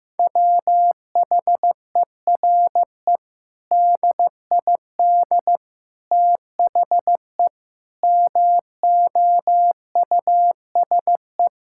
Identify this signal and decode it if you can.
Morse code, heard as a keyed tone, about 15 wpm, 700 Hz. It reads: WHERE DID THE MOUSE
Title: Sentences from Top 1000 Words - Review Part 5 - 15wpm